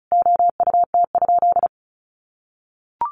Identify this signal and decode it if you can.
Morse code, heard as a keyed tone, about 34 wpm, 700 Hz. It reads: OUT?
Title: Sentences from Top 200 Words 35wpm